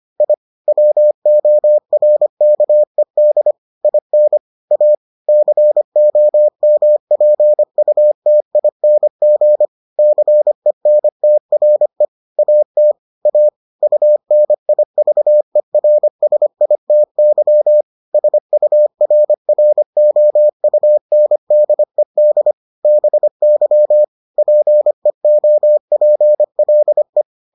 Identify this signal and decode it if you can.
Morse code, heard as a keyed tone, about 25 wpm, 600 Hz. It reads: I WORKED IN A COMPUTING CENTRE AT A UNIVERSITY SURROUNDED BY PEOPLE